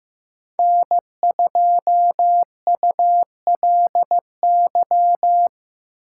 Morse code, heard as a keyed tone, about 15 wpm, 700 Hz. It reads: N2ULY
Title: USA Call Signs - 5 Characters - 8x Character Spacing - 15wpm